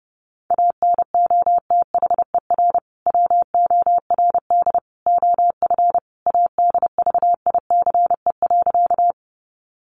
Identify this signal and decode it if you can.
Morse code, heard as a keyed tone, about 30 wpm, 700 Hz. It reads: ANOTHER WORD OF ADVICE.
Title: Sentences from Top 1100 Words 30wpm